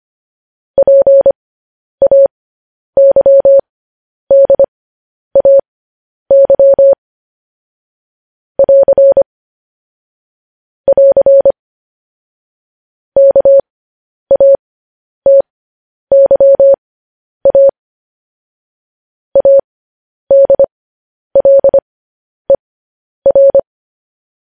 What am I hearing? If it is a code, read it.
Morse code, heard as a keyed tone, about 25 wpm, spaced as for 10 wpm, 550 Hz. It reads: PAYDAY <AR> <AR> KATYA ADLER
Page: news-headlines-in-morse-code-at-10-wpm